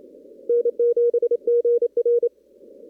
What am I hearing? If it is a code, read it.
Morse code, heard as a keyed tone, about 27 wpm, 480 Hz. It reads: N7GR
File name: CW / kt5j wpx cw-op-k5pi